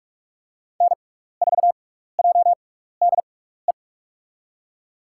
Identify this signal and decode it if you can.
Morse code, heard as a keyed tone, about 47 wpm, spaced as for 16 wpm, 690 Hz. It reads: N4JDE
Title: USA Call Signs - 5 Characters - 6x Character Spacing - 45wpm